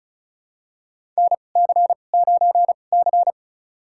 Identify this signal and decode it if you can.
Morse code, heard as a keyed tone, about 34 wpm, spaced as for 25 wpm, 700 Hz. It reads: NC9C
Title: USA Call Signs - 4 Characters - 2x Character Spacing - 35wpm